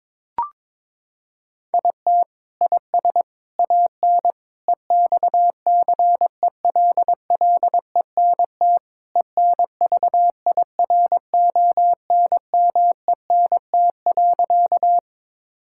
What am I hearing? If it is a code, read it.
Morse code, heard as a keyed tone, about 22 wpm, 700 Hz. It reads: IT IS AN EXCELLENT ENVIRONMENT.
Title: Sentences from Top 1000 Words - Review Part 5 - 22wpm